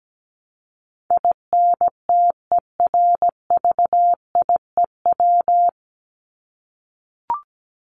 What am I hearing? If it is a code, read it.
Morse code, heard as a keyed tone, about 17 wpm, 700 Hz. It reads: INTERVIEW
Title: Top 1501-1750 Words 17wpm